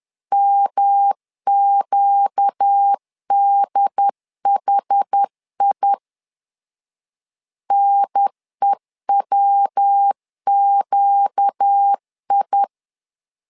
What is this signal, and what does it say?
Morse code, heard as a keyed tone, about 10 wpm, 800 Hz. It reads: MQDHI NEWQI
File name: LEZIONI / 4 Q 10wpm 2210